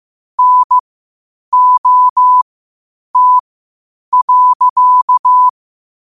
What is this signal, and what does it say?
Morse code, heard as a keyed tone, about 15 wpm, spaced as for 8 wpm, 1000 Hz. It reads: NOT.
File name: qotdfiles / 8wpm